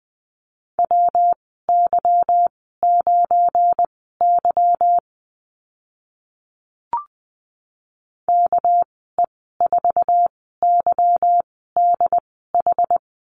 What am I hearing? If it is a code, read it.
Morse code, heard as a keyed tone, about 20 wpm, spaced as for 14 wpm, 700 Hz. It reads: WY9Y KE4YDH
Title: 213 Lesson 34g - Number 8 - Call Signs 20wpm